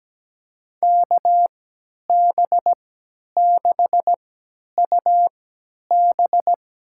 Morse code, heard as a keyed tone, about 17 wpm, spaced as for 10 wpm, 700 Hz. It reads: KB6UB
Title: USA Call Signs - 3x Character Spacing - 17wpm